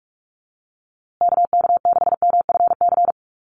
Morse code, heard as a keyed tone, about 44 wpm, 700 Hz. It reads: KK6MFC
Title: USA Call Signs Repeated 2x 45wpm